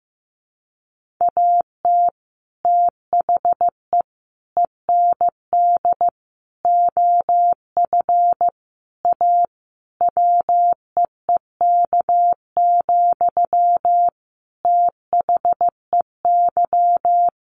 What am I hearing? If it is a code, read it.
Morse code, heard as a keyed tone, about 15 wpm, 700 Hz. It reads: AT THE END OF A WEEK, THEY
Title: Sentences from Top 1400 Words 15wpm